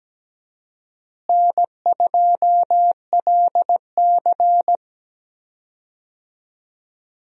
Morse code, heard as a keyed tone, about 17 wpm, 700 Hz. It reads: N2LC
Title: USA Call Signs - 4 Characters - Repeated 3x - 17wpm